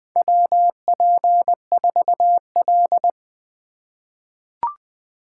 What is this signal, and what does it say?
Morse code, heard as a keyed tone, about 20 wpm, 700 Hz. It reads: WP4L
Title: USA Call Signs - 3x Character Spacing - 20wpm